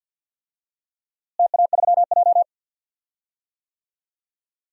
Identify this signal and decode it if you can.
Morse code, heard as a keyed tone, about 51 wpm, 700 Hz. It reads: TA3J
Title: DXCC-100 Call Signs - 4 Characters - Repeated 3x - 50wpm